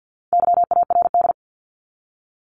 Morse code, heard as a keyed tone, about 48 wpm, 700 Hz. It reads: YARD
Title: Top 1751-2000 Words 50wpm